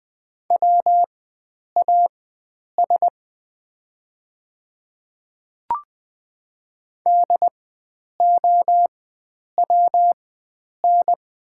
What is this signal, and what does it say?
Morse code, heard as a keyed tone, about 20 wpm, spaced as for 9 wpm, 700 Hz. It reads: WAS DOWN
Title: 4x Character Spacing - Top 100 Words - 20wpm